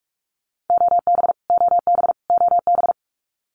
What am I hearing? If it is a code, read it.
Morse code, heard as a keyed tone, about 44 wpm, 700 Hz. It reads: OB OB OB
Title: Abbreviations Repeated 3x - 45wpm